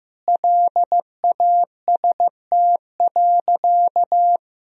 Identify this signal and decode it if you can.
Morse code, heard as a keyed tone, about 16 wpm, 700 Hz. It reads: LAST.
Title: Sentences from Top 1000 Words - Review Part 2 - 15wpm